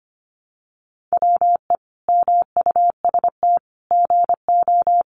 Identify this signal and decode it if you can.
Morse code, heard as a keyed tone, about 25 wpm, 700 Hz. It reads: WE MUST GO
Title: Sentences from Top 800 Words 25wpm